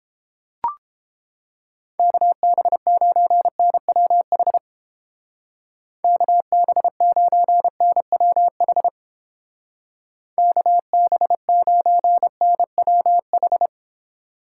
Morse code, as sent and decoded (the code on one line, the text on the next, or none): -.- -... ----. -. .-- .... / -.- -... ----. -. .-- .... / -.- -... ----. -. .-- ....
KB9NWH KB9NWH KB9NWH